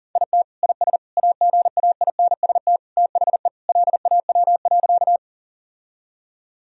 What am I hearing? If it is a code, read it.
Morse code, heard as a keyed tone, about 40 wpm, 700 Hz. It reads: IT IS AGAINST THE LAW.
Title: Sentences from Top 1000 Words - Review Part 3 - 40wpm